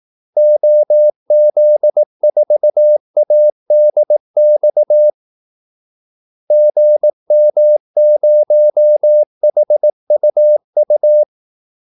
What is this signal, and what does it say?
Morse code, heard as a keyed tone, about 18 wpm, 600 Hz. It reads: OZ4ADX GM0HUU
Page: Kaldesignaler 18 wpm